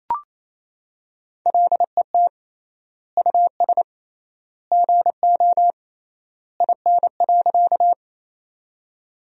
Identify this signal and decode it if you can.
Morse code, heard as a keyed tone, about 27 wpm, 700 Hz. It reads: LET US GO IN.
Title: 3x Word-Spacing - Sentences from Top 300 Words - 28wpm